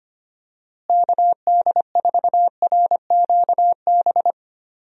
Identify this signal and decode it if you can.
Morse code, heard as a keyed tone, about 25 wpm, 700 Hz. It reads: KD4RQB
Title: USA Call Signs - 6 Characters - Repeated 3x - 25wpm